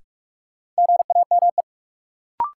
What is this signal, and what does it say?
Morse code, heard as a keyed tone, about 46 wpm, 700 Hz. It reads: GAME
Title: Sentences from Top 500 Words - New Word Drill - 45wpm